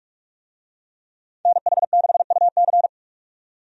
Morse code, heard as a keyed tone, about 46 wpm, 700 Hz. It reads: NH6UC
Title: USA Call Signs - 5 Characters - 45wpm